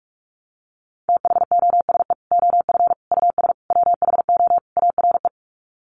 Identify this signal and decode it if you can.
Morse code, heard as a keyed tone, about 44 wpm, 700 Hz. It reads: THOSE OF US WHO ARE
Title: Sets of 5 Words Encore2 45wpm